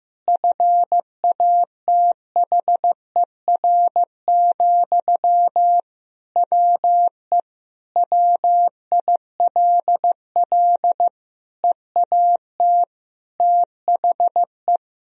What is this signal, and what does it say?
Morse code, heard as a keyed tone, about 15 wpm, 700 Hz. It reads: FATHER, WE WILL EAT THE
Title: Sentences from Top 1000 Words - Review Part 3 - 15wpm